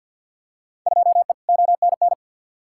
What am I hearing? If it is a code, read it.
Morse code, heard as a keyed tone, about 52 wpm, 700 Hz. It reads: 2E ONN
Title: Field Day Exchanges Encore - 50wpm